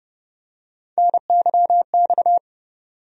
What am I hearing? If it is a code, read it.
Morse code, heard as a keyed tone, about 30 wpm, 700 Hz. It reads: NYX
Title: USA Call Sign Postfixes - 3 Letters 30wpm